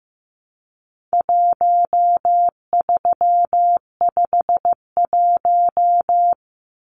Morse code, heard as a keyed tone, about 15 wpm, 700 Hz. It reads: 1351
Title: Numbers - 4-Digit 15wpm